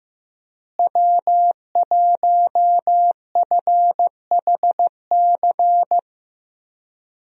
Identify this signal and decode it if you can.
Morse code, heard as a keyed tone, about 15 wpm, 700 Hz. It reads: W1FHC
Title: USA Call Signs - 5 Characters - Repeated 3x - 15wpm